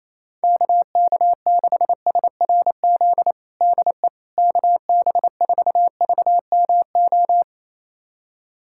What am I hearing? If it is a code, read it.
Morse code, heard as a keyed tone, about 28 wpm, 700 Hz. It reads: KK6SRZ DE KB4VMO
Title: USA Callsign DE Callsign - 6 Characters - 28wpm